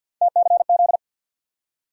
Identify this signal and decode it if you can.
Morse code, heard as a keyed tone, about 53 wpm, 700 Hz. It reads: TCZ
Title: USA Call Sign Postfixes - 3 Letters 50wpm